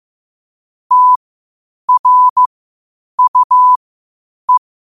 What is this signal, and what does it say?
Morse code, heard as a keyed tone, about 15 wpm, spaced as for 8 wpm, 1000 Hz. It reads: TRUE
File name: qotdfiles / 8wpm